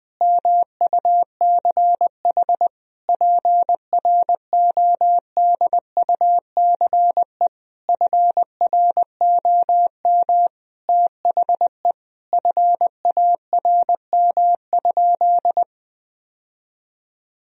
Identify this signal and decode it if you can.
Morse code, heard as a keyed tone, about 20 wpm, 700 Hz. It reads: MUCH PRODUCE FROM THE FARM?
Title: Sentences from Top 1000 Words - Review Part 3 - 20wpm